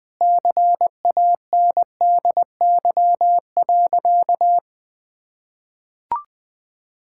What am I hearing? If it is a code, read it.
Morse code, heard as a keyed tone, about 20 wpm, 700 Hz. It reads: CANDY.
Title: Sentences with 4-Word Phrases - 20wpm